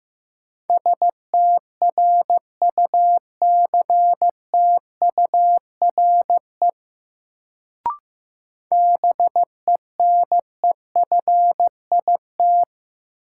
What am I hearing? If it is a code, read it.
Morse code, heard as a keyed tone, about 15 wpm, 700 Hz. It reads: STRUCTURE BENEFIT
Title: Sentences from Top 1000 Words - Review Part 6 - New Word Drill - 15wpm